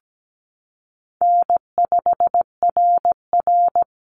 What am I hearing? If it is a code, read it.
Morse code, heard as a keyed tone, about 17 wpm, 700 Hz. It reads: N5RR
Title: Call Signs Repeated 2x - 17wpm